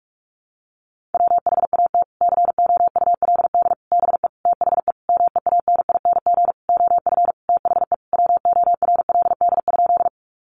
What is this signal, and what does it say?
Morse code, heard as a keyed tone, about 44 wpm, 700 Hz. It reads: WHAT COULD BE THE MEANING OF THE WORLD?
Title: Sentences from Top 900 Words 45wpm